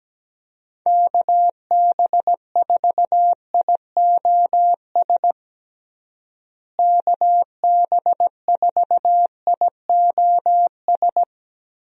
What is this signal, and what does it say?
Morse code, heard as a keyed tone, about 17 wpm, 700 Hz. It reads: KB4IOS KB4IOS
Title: USA Call Signs - 6 Characters - Repeated 2x - 17wpm